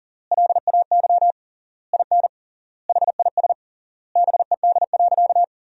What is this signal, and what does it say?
Morse code, heard as a keyed tone, about 37 wpm, 700 Hz. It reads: LAY IN HIS BED.
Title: 3x Word-Spacing - Sentences from Top 500 Words - 40wpm